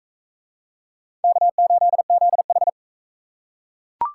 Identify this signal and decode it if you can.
Morse code, heard as a keyed tone, about 38 wpm, 700 Hz. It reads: K8ZH